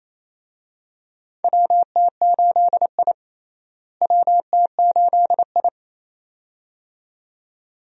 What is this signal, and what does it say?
Morse code, heard as a keyed tone, about 28 wpm, 700 Hz. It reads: WT8I WT8I
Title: USA Call Signs - 4 Characters - Repeated 2x - 28wpm